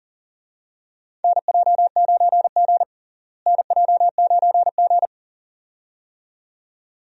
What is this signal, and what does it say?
Morse code, heard as a keyed tone, about 40 wpm, 700 Hz. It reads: NJ9G NJ9G
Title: USA Call Signs - 4 Characters - Repeated 2x - 40wpm